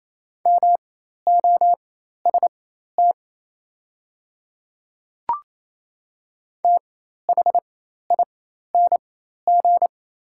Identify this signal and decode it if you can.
Morse code, heard as a keyed tone, about 27 wpm, spaced as for 13 wpm, 700 Hz. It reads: MOST THING
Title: 4x Character Spacing - Top 100 Words - 28wpm